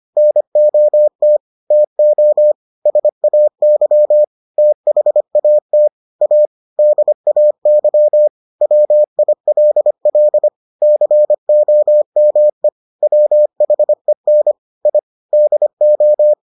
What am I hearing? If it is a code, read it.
Morse code, heard as a keyed tone, about 25 wpm, 600 Hz. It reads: NOT TO SAY THAT A DAY WILL COME WHEN I DO